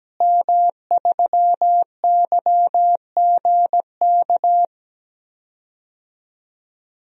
Morse code, heard as a keyed tone, about 17 wpm, 700 Hz. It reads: M3YGK
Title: DXCC-100 Call Signs - 5 Characters - Repeated 2x - 17wpm